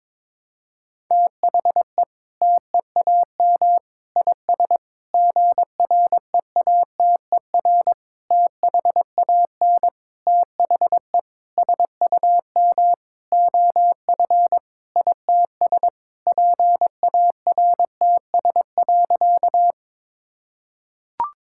Morse code, heard as a keyed tone, about 22 wpm, 700 Hz. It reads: THE TEAM IS GREATER THAN THE SUM OF ITS PARTS.
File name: practice / 22wpm / Sentences with 8-Word Phrases - 22wpm